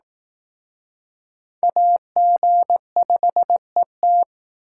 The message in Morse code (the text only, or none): AG5ET